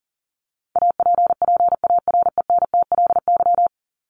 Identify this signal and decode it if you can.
Morse code, heard as a keyed tone, about 40 wpm, 700 Hz. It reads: APPARENTLY
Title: LY Suffix - Top 100 Words with Base Words 40wpm